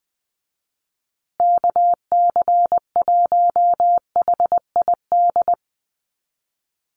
Morse code, heard as a keyed tone, about 20 wpm, 700 Hz. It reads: KC1HID
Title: USA Call Signs Repeated 3x 20wpm